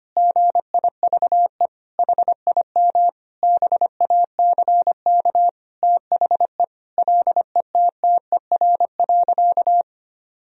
Morse code, heard as a keyed tone, about 25 wpm, 700 Hz. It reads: GIVE HIM BACK THE LETTER.
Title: Sentences from Top 1000 Words - Review Part 4 - 25wpm